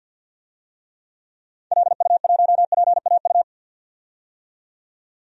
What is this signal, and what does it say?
Morse code, heard as a keyed tone, about 52 wpm, 700 Hz. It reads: LU1PAU
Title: DXCC-100 Call Signs - 6 Characters - 50wpm